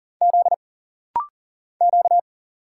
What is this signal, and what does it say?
Morse code, heard as a keyed tone, about 40 wpm, 700 Hz. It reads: Z Q
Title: Single Letters - Rapid-Fire 40wpm